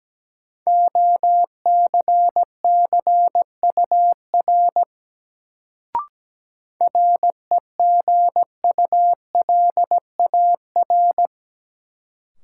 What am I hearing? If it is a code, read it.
Morse code, heard as a keyed tone, about 17 wpm, 700 Hz. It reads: OCCUR REGULAR
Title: Sentences from Top 900 Words - New Word Drill - 17wpm